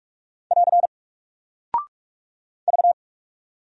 Morse code, heard as a keyed tone, about 44 wpm, 700 Hz. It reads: <AR> V
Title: Single Character (Letter:Number:Punctuation:Prosign) Alt 45wpm